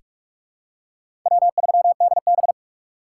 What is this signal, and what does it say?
Morse code, heard as a keyed tone, about 46 wpm, 700 Hz. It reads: W3DB
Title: USA Call Signs - 3x Character Spacing - 45wpm